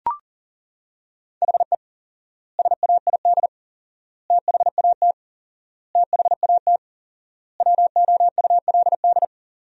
Morse code, heard as a keyed tone, about 37 wpm, 700 Hz. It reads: HE SAID THAT THAT WOULD
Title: 4x Word-Spacing - Sentences from Top 200 Words - 40wpm